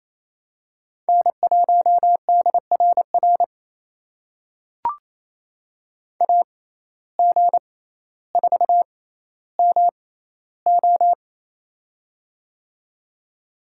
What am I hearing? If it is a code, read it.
Morse code, heard as a keyed tone, about 28 wpm, spaced as for 14 wpm, 700 Hz. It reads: N1DRR AG4MO
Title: USA Call Signs - 5 Characters - 6x Character Spacing - 28wpm